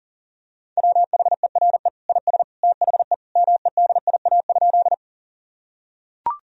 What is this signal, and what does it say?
Morse code, heard as a keyed tone, about 40 wpm, 700 Hz. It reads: WHERE IS THE MEDIA?
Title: Sentences from Top 1000 Words - Review Part 4 - 40wpm